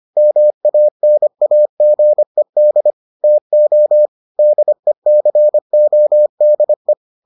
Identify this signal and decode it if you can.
Morse code, heard as a keyed tone, about 25 wpm, 600 Hz. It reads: MANAGED TO DECODE